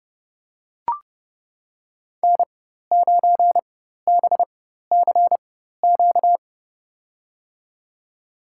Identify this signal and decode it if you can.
Morse code, heard as a keyed tone, about 30 wpm, spaced as for 14 wpm, 700 Hz. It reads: N9BCQ
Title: USA Call Signs - 5 Characters - 4x Character Spacing - 30wpm